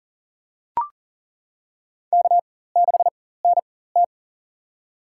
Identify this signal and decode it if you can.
Morse code, heard as a keyed tone, about 40 wpm, spaced as for 19 wpm, 700 Hz. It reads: K6NT
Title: USA Call Signs - 4 Characters - 4x Character Spacing - 40wpm